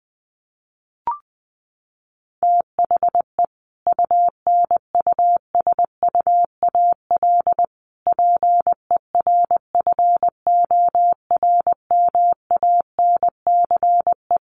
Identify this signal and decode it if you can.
Morse code, heard as a keyed tone, about 20 wpm, 700 Hz. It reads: THE UNUSUAL PERFORMANCE
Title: Sentences from Top 1400 Words 20wpm